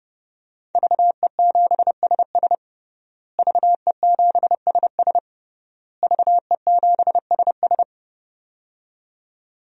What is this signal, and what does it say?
Morse code, heard as a keyed tone, about 32 wpm, 700 Hz. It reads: VE7SS VE7SS VE7SS
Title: DXCC-100 Call Signs - 5 Characters - Repeated 3x - 30wpm